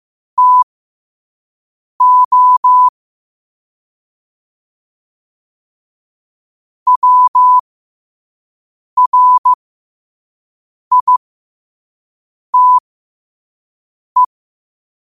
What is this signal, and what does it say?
Morse code, heard as a keyed tone, about 15 wpm, spaced as for 5 wpm, 1000 Hz. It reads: TO WRITE
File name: qotdfiles / 5wpm